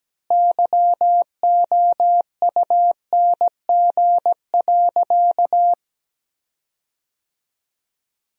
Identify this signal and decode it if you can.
Morse code, heard as a keyed tone, about 17 wpm, 700 Hz. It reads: YOUNG.
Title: 4x Word-Spacing - Sentences from Top 400 Words - 17wpm